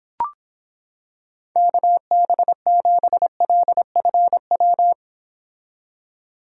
Morse code, heard as a keyed tone, about 26 wpm, 700 Hz. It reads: KB7LFW